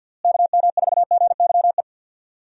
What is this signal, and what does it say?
Morse code, heard as a keyed tone, about 52 wpm, 700 Hz. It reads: KM4GYE
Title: USA Call Signs - 6 Characters - Repeated 3x - 50wpm